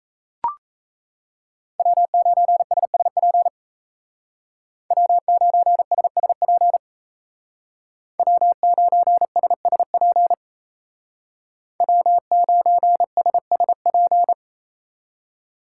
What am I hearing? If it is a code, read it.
Morse code, heard as a keyed tone, about 35 wpm, 700 Hz. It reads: W9SSP W9SSP W9SSP W9SSP